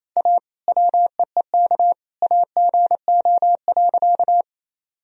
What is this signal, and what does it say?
Morse code, heard as a keyed tone, about 27 wpm, 700 Hz. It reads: A WEEK AGO.
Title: Sentences from Top 1000 Words - Review Part 3 - 28wpm